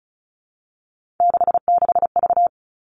Morse code, heard as a keyed tone, about 36 wpm, 700 Hz. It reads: 66V
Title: Three Challenging Letter-Number Combinations 35wpm